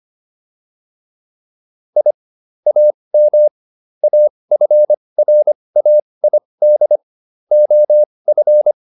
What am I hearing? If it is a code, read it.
Morse code, heard as a keyed tone, about 25 wpm, spaced as for 20 wpm, 600 Hz. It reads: I AM AFRAID OF